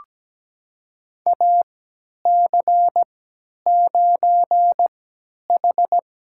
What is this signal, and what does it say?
Morse code, heard as a keyed tone, about 17 wpm, spaced as for 10 wpm, 700 Hz. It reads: AC9H
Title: USA Call Signs - 3x Character Spacing - 17wpm